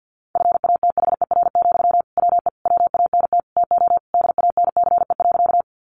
Morse code, heard as a keyed tone, about 50 wpm, 700 Hz. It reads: FATHER, WE WANT TO DANCE.
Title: Sentences from Top 700 Words 50wpm